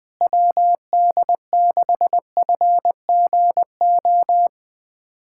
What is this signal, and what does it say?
Morse code, heard as a keyed tone, about 21 wpm, 700 Hz. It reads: WD6FGO
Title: USA Callsign DE Callsign 20wpm